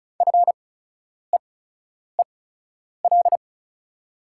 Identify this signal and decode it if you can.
Morse code, heard as a keyed tone, about 36 wpm, spaced as for 10 wpm, 700 Hz. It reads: FEEL
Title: 8x Character Spacing - Top 500 Words - 35wpm